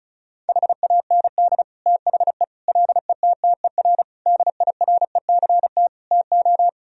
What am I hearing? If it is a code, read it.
Morse code, heard as a keyed tone, about 34 wpm, 700 Hz. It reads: HAND THE LETTER DIRECT TO
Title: Sentences from Top 1000 Words - Review Part 2 - 35wpm